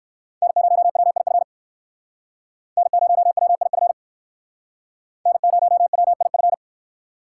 Morse code, heard as a keyed tone, about 56 wpm, 700 Hz. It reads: N0WIF N0WIF N0WIF